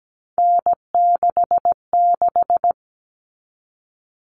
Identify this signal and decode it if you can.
Morse code, heard as a keyed tone, about 17 wpm, 700 Hz. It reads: N66
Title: All Cut Numbers - 3-Digit 17wpm